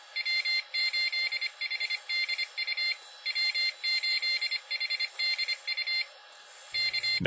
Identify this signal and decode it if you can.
Morse code, heard as a keyed tone, about 26 wpm, 2100 Hz. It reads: W8HDU W8HDU K